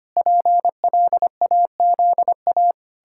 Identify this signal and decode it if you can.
Morse code, heard as a keyed tone, about 25 wpm, 700 Hz. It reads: PLAZA
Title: Pangram Challenge v2 - 25wpm